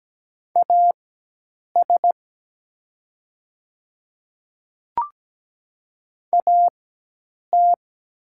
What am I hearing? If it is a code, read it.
Morse code, heard as a keyed tone, about 17 wpm, spaced as for 8 wpm, 700 Hz. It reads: AS AT
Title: 4x Character Spacing - Top 100 Words - 17wpm